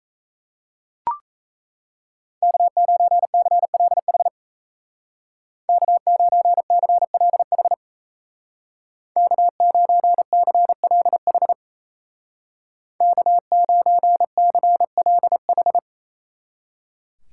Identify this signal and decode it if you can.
Morse code, heard as a keyed tone, about 33 wpm, 700 Hz. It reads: K9CLH K9CLH K9CLH K9CLH
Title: Speed Racing - USA Call Signs 28wpm